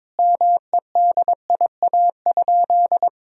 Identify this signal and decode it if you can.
Morse code, heard as a keyed tone, about 22 wpm, 700 Hz. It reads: MEDIA?
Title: Sentences from Top 1000 Words - Review Part 4 - 22wpm